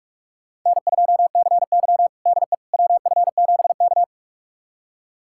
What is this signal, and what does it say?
Morse code, heard as a keyed tone, about 42 wpm, 700 Hz. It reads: N2CY DE WF7K
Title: USA Callsign DE Callsign - 4 Characters - 45wpm